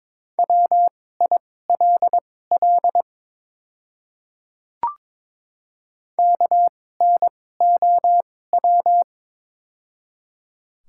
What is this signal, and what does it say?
Morse code, heard as a keyed tone, about 22 wpm, spaced as for 16 wpm, 690 Hz. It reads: WILL KNOW
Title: 2x Character Spacing - Top 100 Words - 22wpm